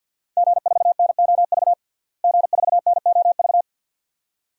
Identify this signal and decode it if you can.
Morse code, heard as a keyed tone, about 53 wpm, 700 Hz. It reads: G4NOV G4NOV
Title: DXCC-100 Call Signs - 5 Characters - Repeated 2x - 50wpm